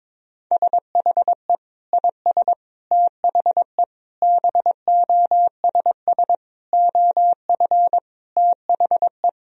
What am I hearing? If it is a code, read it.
Morse code, heard as a keyed tone, about 22 wpm, 700 Hz. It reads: SHE IS THE BOSS OF THE
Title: Sentences from Top 1000 Words - Review Part 4 - 22wpm